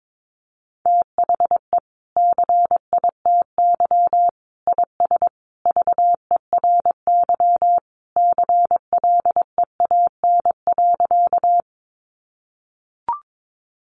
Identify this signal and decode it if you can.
Morse code, heard as a keyed tone, about 22 wpm, 700 Hz. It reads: THE CITY IS VERY CLEAN.
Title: Easy Phrases 22wpm